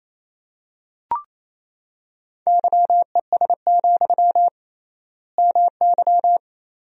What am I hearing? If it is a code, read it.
Morse code, heard as a keyed tone, about 28 wpm, 700 Hz. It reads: YES, MY
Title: 3x Word-Spacing - Sentences from Top 500 Words - 28wpm